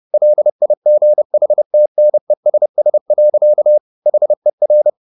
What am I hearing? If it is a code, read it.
Morse code, heard as a keyed tone, about 30 wpm, 600 Hz. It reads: LIGHTNESS. HER